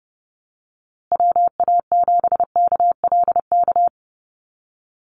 Morse code, heard as a keyed tone, about 30 wpm, 700 Hz. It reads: WA7KLK
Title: USA Call Signs Repeated 3x 30wpm